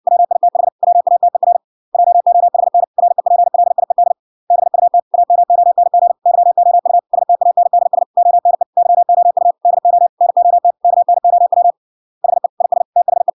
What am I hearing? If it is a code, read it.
Morse code, heard as a keyed tone, about 56 wpm, 700 Hz. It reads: WITH WATER JOHN REPLIED BUT AMONG YOU STANDS ONE YOU DO NOT KNOW HE IS THE